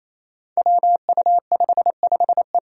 Morse code, heard as a keyed tone, about 28 wpm, 700 Hz. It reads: WU55E
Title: Five Challenging Letter-Number Combinations 28wpm